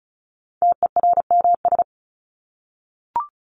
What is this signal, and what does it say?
Morse code, heard as a keyed tone, about 34 wpm, 700 Hz. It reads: TERMS
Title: ES:S Suffix - Top 100 Words with Base Words 35wpm